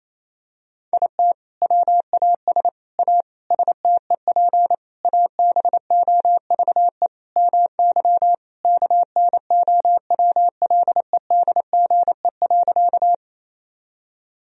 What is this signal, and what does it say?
Morse code, heard as a keyed tone, about 28 wpm, 700 Hz. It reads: IT WAS A STEP ABOVE MY KNOWLEDGE.
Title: Sentences from Top 1000 Words - Review Part 3 - 28wpm